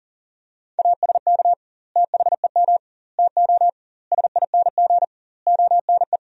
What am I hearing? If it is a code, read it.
Morse code, heard as a keyed tone, about 40 wpm, 700 Hz. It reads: ASK THEM TO SING ONE
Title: 2x Word-Spacing - Sentences from Top 400 Words - 40wpm